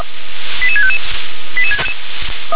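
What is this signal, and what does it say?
Five-tone selective call (ZVEI), heard as quick runs of five tones, tones of about 70 ms. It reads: 88655
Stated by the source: ZVEI five-tone selective call 88768 received by SDR.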